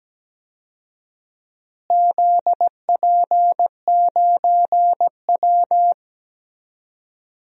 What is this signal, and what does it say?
Morse code, heard as a keyed tone, about 17 wpm, 700 Hz. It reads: ZP9W
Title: DXCC-100 Call Signs - 4 Characters - Repeated 3x - 17wpm